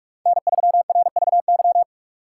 Morse code, heard as a keyed tone, about 67 wpm, spaced as for 56 wpm, 700 Hz. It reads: N3RVY
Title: Speed Racing - USA Call Signs 45wpm